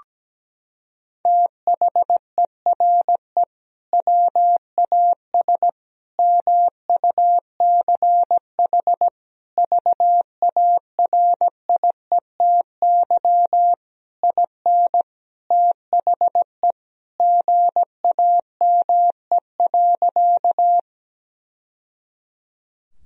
Morse code, heard as a keyed tone, about 17 wpm, 700 Hz. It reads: THERE WAS MUCH VARIETY IN THE GAME.
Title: Sentences from Top 1000 Words - Review Part 4 - 17wpm